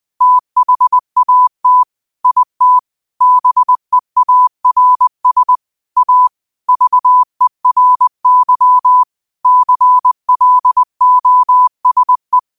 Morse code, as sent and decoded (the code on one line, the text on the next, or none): - .... .- - / .. - / -... .- .-. ... / .- / ...- . .-. -.-- / -.-. .-.. --- ... .
THAT IT BEARS A VERY CLOSE